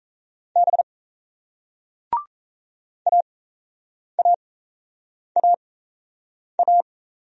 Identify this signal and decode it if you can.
Morse code, heard as a keyed tone, about 38 wpm, spaced as for 8 wpm, 700 Hz. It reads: B AAAA